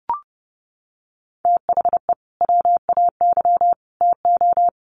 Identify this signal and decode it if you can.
Morse code, heard as a keyed tone, about 30 wpm, 700 Hz. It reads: THE WAY TO
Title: Sets of 3 Words 30wpm